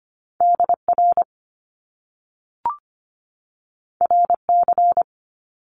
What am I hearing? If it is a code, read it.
Morse code, heard as a keyed tone, about 25 wpm, 700 Hz. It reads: DR RC
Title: Top Two Letter Combinations in Words 25wpm